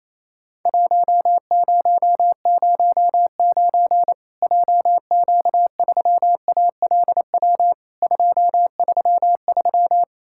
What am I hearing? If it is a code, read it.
Morse code, heard as a keyed tone, about 28 wpm, 700 Hz. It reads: 1009 JQ3ALW 233
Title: Worked All Europe DX Contest - QTC Exchange - 28wpm